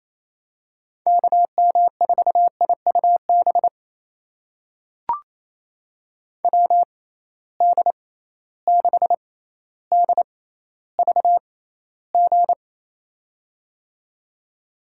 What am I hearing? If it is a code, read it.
Morse code, heard as a keyed tone, about 28 wpm, spaced as for 14 wpm, 700 Hz. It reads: KM4IUB WD6DVG
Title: USA Call Signs - 6 Characters - 6x Character Spacing - 28wpm